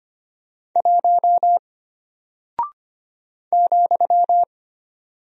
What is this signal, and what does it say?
Morse code, heard as a keyed tone, about 25 wpm, 700 Hz. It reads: 1 ,
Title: Single Character (Letter:Number:Punctuation:Prosign) Alt 25wpm